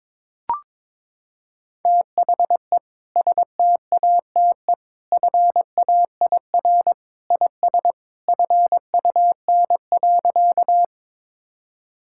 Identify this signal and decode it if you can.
Morse code, heard as a keyed tone, about 22 wpm, 700 Hz. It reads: THE STATE FAIR IS FUN.
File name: practice / 22wpm / Sentences with 2-Word Phrases - 22wpm